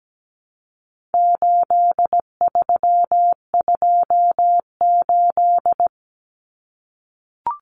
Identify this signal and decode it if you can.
Morse code, heard as a keyed tone, about 17 wpm, 700 Hz. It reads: 8328
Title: Numbers - 4-Digit 17wpm